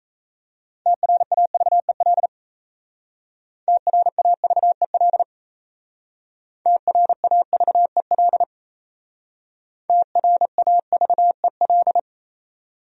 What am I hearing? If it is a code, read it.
Morse code, heard as a keyed tone, about 36 wpm, 700 Hz. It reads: TRAVEL TRAVEL TRAVEL TRAVEL